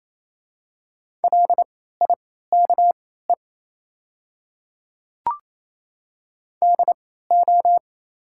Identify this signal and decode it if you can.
Morse code, heard as a keyed tone, about 28 wpm, spaced as for 16 wpm, 690 Hz. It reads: LIKE DO
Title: 3x Character Spacing - Top 100 Words - 28wpm